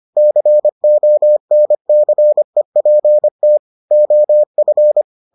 Morse code, heard as a keyed tone, about 25 wpm, 600 Hz. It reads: CONCEPT OF